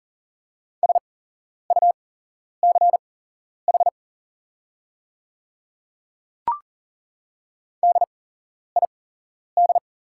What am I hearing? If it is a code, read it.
Morse code, heard as a keyed tone, about 40 wpm, spaced as for 11 wpm, 690 Hz. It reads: SUCH DID
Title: 8x Character Spacing - Top 200 Words - 40wpm